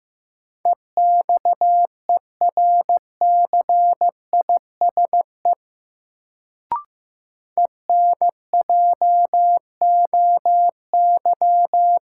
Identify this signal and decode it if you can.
Morse code, heard as a keyed tone, about 15 wpm, 700 Hz. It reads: EXERCISE ENJOY
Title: Sentences from Top 1000 Words - Review Part 4 - New Word Drill - 15wpm